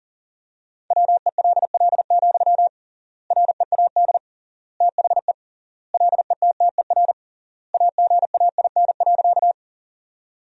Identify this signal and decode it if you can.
Morse code, heard as a keyed tone, about 40 wpm, 700 Hz. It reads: WELL, READ THE LETTER AGAIN.
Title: 3x Word-Spacing - Sentences from Top 300 Words - 40wpm